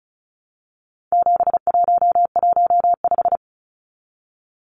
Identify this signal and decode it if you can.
Morse code, heard as a keyed tone, about 34 wpm, 700 Hz. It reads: 7115
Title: Numbers - 4-Digit 35wpm